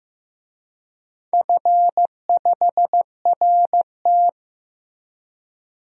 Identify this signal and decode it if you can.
Morse code, heard as a keyed tone, about 16 wpm, 700 Hz. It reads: F5RT
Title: DXCC-100 Call Signs - 4 Characters - Repeated 2x - 15wpm